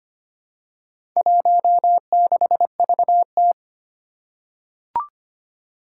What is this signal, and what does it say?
Morse code, heard as a keyed tone, about 25 wpm, 700 Hz. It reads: 16VT
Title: Four Challenging Letter-Number Combinations 25wpm